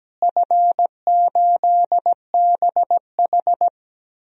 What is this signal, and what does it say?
Morse code, heard as a keyed tone, about 17 wpm, 700 Hz. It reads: F8BH
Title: DXCC-100 Call Signs - 4 Characters - Repeated 3x - 17wpm